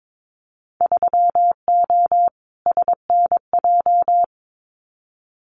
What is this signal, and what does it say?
Morse code, heard as a keyed tone, about 22 wpm, 700 Hz. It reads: 3O SNJ
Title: Winter Field Day Exchanges Encore 22wpm